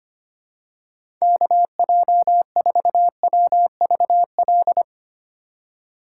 Morse code, heard as a keyed tone, about 25 wpm, 700 Hz. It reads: KJ4WVL
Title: USA Call Signs - 6 Characters - 4x Character Spacing - 25wpm